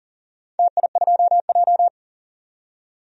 Morse code, heard as a keyed tone, about 40 wpm, 700 Hz. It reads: TI2J
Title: DXCC-100 Call Signs - 4 Characters - 40wpm